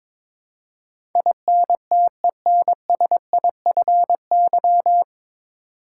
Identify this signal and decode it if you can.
Morse code, heard as a keyed tone, about 22 wpm, 700 Hz. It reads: INTENSIFY
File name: 9 Letter Words - First 3 Letters Given - 22wpm